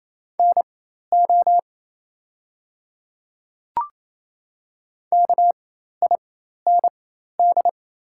Morse code, heard as a keyed tone, about 28 wpm, spaced as for 13 wpm, 700 Hz. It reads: NO KIND
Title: 4x Character Spacing - Top 200 Words - 28wpm